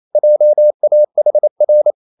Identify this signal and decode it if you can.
Morse code, heard as a keyed tone, about 28 wpm, 600 Hz. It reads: JAHR
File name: marienkind9 28wpm